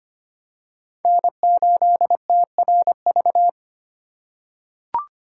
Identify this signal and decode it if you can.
Morse code, heard as a keyed tone, about 25 wpm, 700 Hz. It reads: N8TRV
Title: USA Call Signs - 5 Characters - 4x Character Spacing - 25wpm